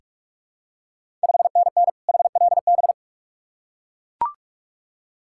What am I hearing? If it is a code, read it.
Morse code, heard as a keyed tone, about 44 wpm, 690 Hz. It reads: 5NN HLB
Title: Ontario QSO Party Exchange 45wpm